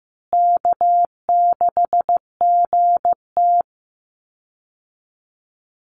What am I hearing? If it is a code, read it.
Morse code, heard as a keyed tone, about 15 wpm, 700 Hz. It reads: K6GT
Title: Call Signs Repeated 3x - 15wpm